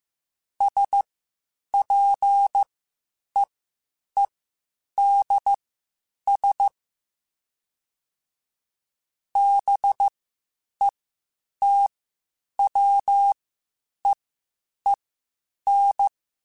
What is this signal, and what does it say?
Morse code, heard as a keyed tone, about 15 wpm, spaced as for 8 wpm, 800 Hz. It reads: SPEEDS BETWEEN
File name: Spaced Characters / Spaced Characters-mp3s / Weather 4 10 sp @13